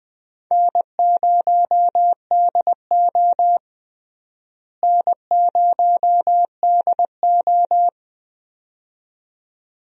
Morse code, heard as a keyed tone, about 20 wpm, 700 Hz. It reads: N0DO N0DO
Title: USA Call Signs - 4 Characters - Repeated 2x - 20wpm